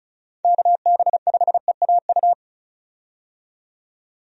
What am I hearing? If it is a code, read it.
Morse code, heard as a keyed tone, about 34 wpm, 700 Hz. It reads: KB5EAU
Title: USA Call Signs - 6 Characters - 35wpm